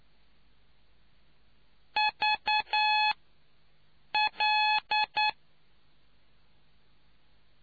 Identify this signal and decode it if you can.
Morse code, heard as a keyed tone, about 9 wpm, spaced as for 6 wpm, 850 Hz. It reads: VL